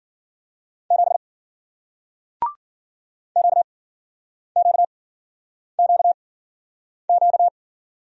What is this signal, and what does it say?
Morse code, heard as a keyed tone, about 54 wpm, spaced as for 9 wpm, 700 Hz. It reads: 7 QQQQ